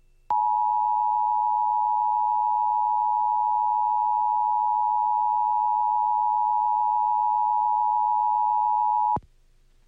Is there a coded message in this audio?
Yes: Emergency Alert System (SAME) tones (the 853 and 960 Hz attention tone) from 0.3 to 9.2 s.